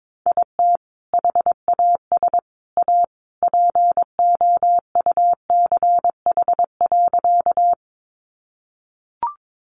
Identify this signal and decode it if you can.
Morse code, heard as a keyed tone, about 22 wpm, 700 Hz. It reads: IT HAS A POUCH.
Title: Easy Phrases 22wpm